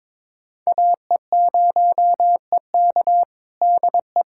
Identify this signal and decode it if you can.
Morse code, heard as a keyed tone, about 22 wpm, 700 Hz. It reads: AE0EK DE
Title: USA Callsign DE Callsign - 5 Characters - 22wpm